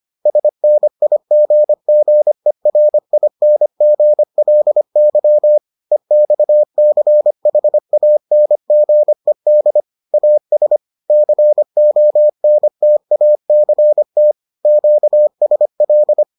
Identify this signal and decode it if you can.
Morse code, heard as a keyed tone, about 25 wpm, 600 Hz. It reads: SNIGGERINGLY EXCHANGED AS CONTACT QSL